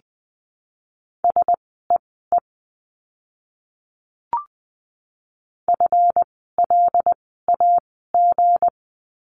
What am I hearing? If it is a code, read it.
Morse code, heard as a keyed tone, about 20 wpm, spaced as for 15 wpm, 700 Hz. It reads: SEE FLAG
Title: 160 Lesson 26f - Letter g - Words 20wpm